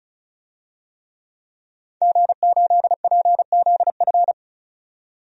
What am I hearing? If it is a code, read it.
Morse code, heard as a keyed tone, about 36 wpm, 700 Hz. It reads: G8PZF
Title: DXCC-100 Call Signs - 5 Characters - 35wpm